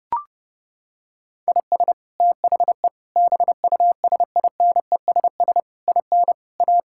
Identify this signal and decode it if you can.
Morse code, heard as a keyed tone, about 32 wpm, 700 Hz. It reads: IS THE BUSINESS IN A
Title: Sentences from Top 1000 Words - Review Part 5 - 30wpm